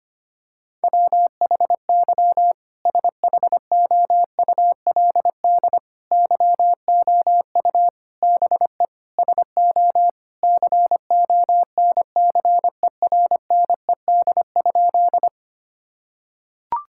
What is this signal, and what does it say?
Morse code, heard as a keyed tone, about 25 wpm, 700 Hz. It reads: WHY SHOULD YOU BE SO CONCERNED?
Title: Sentences from Top 1000 Words - Review Part 6 - 25wpm